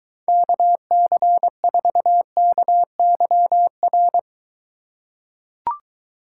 Morse code, heard as a keyed tone, about 23 wpm, 700 Hz. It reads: KC4KYR